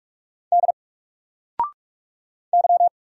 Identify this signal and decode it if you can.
Morse code, heard as a keyed tone, about 44 wpm, 700 Hz. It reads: D Y
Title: Single Letter-Number 45wpm